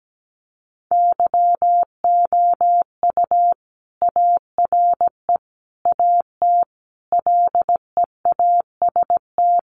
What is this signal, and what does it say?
Morse code, heard as a keyed tone, about 17 wpm, 700 Hz. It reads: YOU ARE AT LEAST
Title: Sentences from Top 1100 Words 17wpm